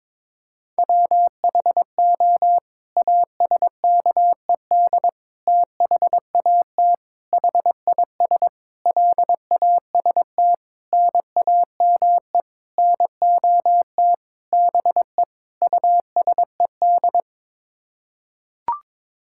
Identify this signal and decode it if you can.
Morse code, heard as a keyed tone, about 22 wpm, 700 Hz. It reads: WHO ASKED THAT HIS LAST NAME NOT BE USED
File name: Sets of 9 Words - 22wpm